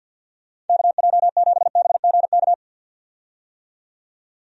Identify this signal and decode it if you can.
Morse code, heard as a keyed tone, about 51 wpm, 700 Hz. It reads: KJ7BGK
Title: USA Call Signs - 6 Characters - 50wpm